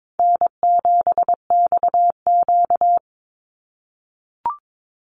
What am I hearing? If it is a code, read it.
Morse code, heard as a keyed tone, about 22 wpm, 700 Hz. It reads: N7XQ
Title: USA Call Signs 22wpm